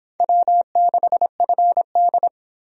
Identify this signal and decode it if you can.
Morse code, heard as a keyed tone, about 24 wpm, 700 Hz. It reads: W6FD